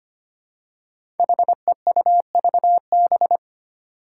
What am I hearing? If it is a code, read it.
Morse code, heard as a keyed tone, about 25 wpm, 700 Hz. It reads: HEUVB
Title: Five Challenging Letter-Number Combinations 25wpm